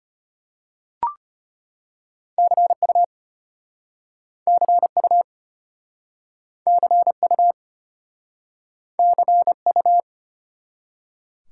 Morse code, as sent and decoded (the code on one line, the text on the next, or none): -.-. ..- / -.-. ..- / -.-. ..- / -.-. ..-
CU CU CU CU